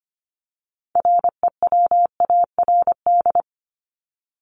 Morse code, heard as a keyed tone, about 25 wpm, 700 Hz. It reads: REWARD
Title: Top 1751-2000 Words 25wpm